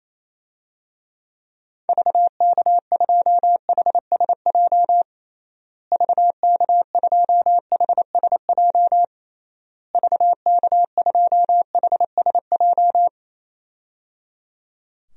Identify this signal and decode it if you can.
Morse code, heard as a keyed tone, about 28 wpm, 700 Hz. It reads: VK2HSJ VK2HSJ VK2HSJ
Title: DXCC-100 Call Signs - 6 Characters - Repeated 3x - 28wpm